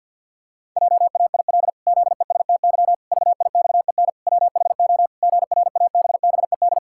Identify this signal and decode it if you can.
Morse code, heard as a keyed tone, about 50 wpm, 700 Hz. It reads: JAIL ZESTY VIXEN WHO GRABBED
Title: Pangram Challenge v2 - 50wpm